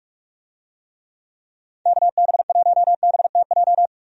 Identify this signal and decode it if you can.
Morse code, heard as a keyed tone, about 46 wpm, 700 Hz. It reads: KB1BTJ
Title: USA Call Signs - 6 Characters - 45wpm